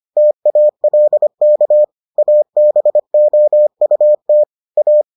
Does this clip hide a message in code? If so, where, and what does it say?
Yes: Morse code from 0.2 to 5.0 s, heard as a keyed tone, about 25 wpm, 600 Hz. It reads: TALK ABOUT A